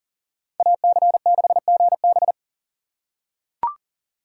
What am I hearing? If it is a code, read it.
Morse code, heard as a keyed tone, about 37 wpm, 700 Hz. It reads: AC6GB
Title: USA Call Signs - 5 Characters - 4x Character Spacing - 40wpm